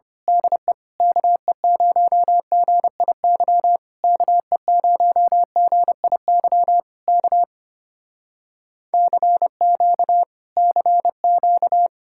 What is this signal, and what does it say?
Morse code, heard as a keyed tone, about 25 wpm, 700 Hz. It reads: DE KE0GIY KE0GIY K CQ CQ
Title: Speed-Racing Common QSO Phrases - 25wpm